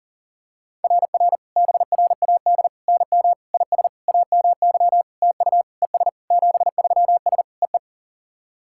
Morse code, heard as a keyed tone, about 44 wpm, 690 Hz. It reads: RR BRAD NM IS AMY TU ES 73S EE
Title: SOTA QSO Practice 40wpm